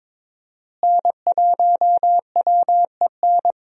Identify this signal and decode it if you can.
Morse code, heard as a keyed tone, about 21 wpm, 700 Hz. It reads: N1WEN
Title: USA Call Signs - 5 Characters - Repeated 2x - 22wpm